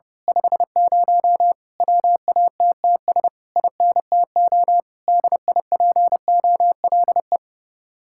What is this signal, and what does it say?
Morse code, heard as a keyed tone, about 28 wpm, 700 Hz. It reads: 50 WATTS INTO DIPOLE
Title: Speed-Racing Common QSO Phrases Encore - 30wpm